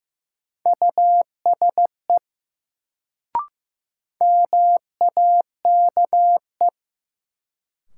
Morse code, heard as a keyed tone, about 15 wpm, 700 Hz. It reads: USE MAKE
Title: Sentences from Top 100 Words - New Word Drill - 15wpm